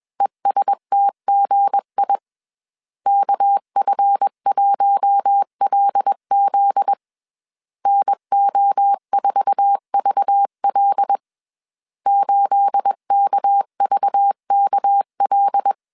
Morse code, heard as a keyed tone, about 21 wpm, 800 Hz. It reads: ESTGI KF1LZ NO4VL 8KVKL
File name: CW / DET / 1 9 1500C S C 20wpm 2310